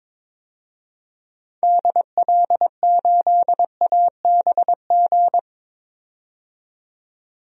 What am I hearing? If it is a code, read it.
Morse code, heard as a keyed tone, about 22 wpm, 700 Hz. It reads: DL8ABG
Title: DXCC-100 Call Signs - 6 Characters - 22wpm